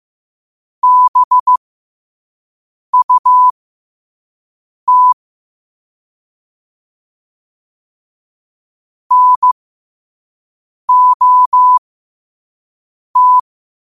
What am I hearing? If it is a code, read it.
Morse code, heard as a keyed tone, about 15 wpm, spaced as for 5 wpm, 1000 Hz. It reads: BUT NOT